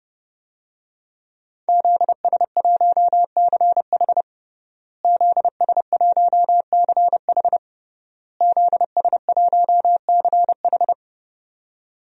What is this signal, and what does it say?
Morse code, heard as a keyed tone, about 30 wpm, 700 Hz. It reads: ZS1CH ZS1CH ZS1CH
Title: DXCC-100 Call Signs - 5 Characters - Repeated 3x - 30wpm